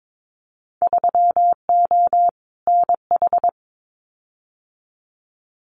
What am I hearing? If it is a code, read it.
Morse code, heard as a keyed tone, about 22 wpm, 700 Hz. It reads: 3O NH
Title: Winter Field Day Exchanges Encore 22wpm